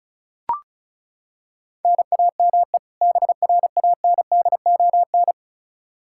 Morse code, heard as a keyed tone, about 36 wpm, 700 Hz. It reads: NAME BRANDON
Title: K1USN SST QSO Practice 35wpm